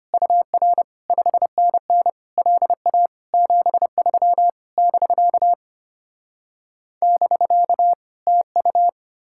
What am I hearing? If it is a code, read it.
Morse code, heard as a keyed tone, about 33 wpm, 700 Hz. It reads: UR 5NN LA 73 <BK> <BK> TU